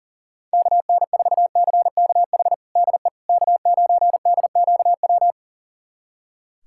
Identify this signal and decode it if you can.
Morse code, heard as a keyed tone, about 40 wpm, 700 Hz. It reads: KN4CKH DE K9DQW
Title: USA Callsign DE Callsign 40wpm